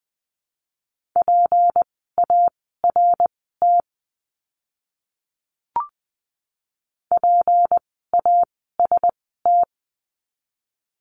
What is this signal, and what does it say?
Morse code, heard as a keyed tone, about 20 wpm, spaced as for 15 wpm, 700 Hz. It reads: PART PAST
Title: 153 Lesson 25f - Letter p - Words 20wpm